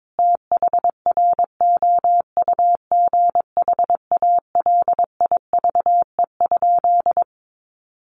Morse code, heard as a keyed tone, about 22 wpm, 700 Hz. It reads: THROUGH ALIVE?
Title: A Sound Of Thunder 22wpm